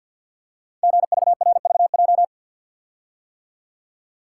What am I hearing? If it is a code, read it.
Morse code, heard as a keyed tone, about 46 wpm, 700 Hz. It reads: GVRVJ
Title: Code Groups - 5 Letter:Numbers - 50wpm